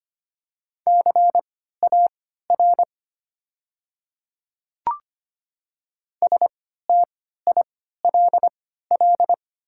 Morse code, heard as a keyed tone, about 25 wpm, spaced as for 14 wpm, 700 Hz. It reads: CAR STILL
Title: 3x Character Spacing - Top 300 Words - 25wpm